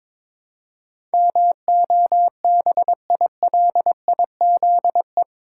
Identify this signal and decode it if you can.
Morse code, heard as a keyed tone, about 21 wpm, 700 Hz. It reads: MOBILIZE
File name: practice / 22wpm / Eight Letter Words - 22wpm